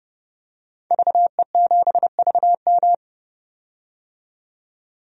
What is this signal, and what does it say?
Morse code, heard as a keyed tone, about 30 wpm, 700 Hz. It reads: VE7VM
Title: DXCC-100 Call Signs - 5 Characters - 30wpm